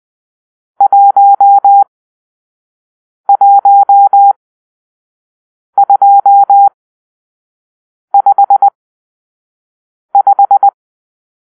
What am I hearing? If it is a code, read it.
Morse code, heard as a keyed tone, about 20 wpm, spaced as for 5 wpm, 800 Hz. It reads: 11255
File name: arrl / Grp-6 Drill-3